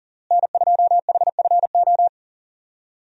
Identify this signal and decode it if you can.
Morse code, heard as a keyed tone, about 40 wpm, 700 Hz. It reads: N2HFO
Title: USA Call Signs - 5 Characters - Repeated 2x - 40wpm